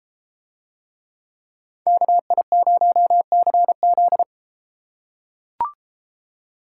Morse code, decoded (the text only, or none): KI0CZ